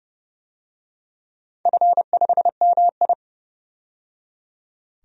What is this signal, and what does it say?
Morse code, heard as a keyed tone, about 30 wpm, 700 Hz. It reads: F5MI
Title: DXCC-100 Call Signs - 4 Characters - 30wpm